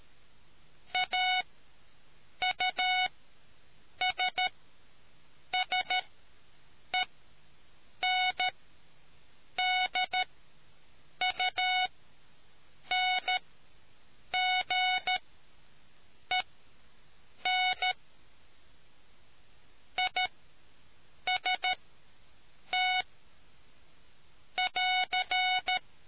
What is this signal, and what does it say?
Morse code, heard as a keyed tone, about 13 wpm, spaced as for 6 wpm, 740 Hz. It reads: AUSSENDUNGEN IST <AR>